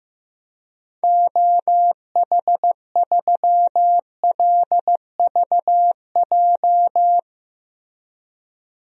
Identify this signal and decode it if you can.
Morse code, heard as a keyed tone, about 15 wpm, 700 Hz. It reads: OH3LVJ
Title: DXCC-100 Call Signs - 6 Characters - 15wpm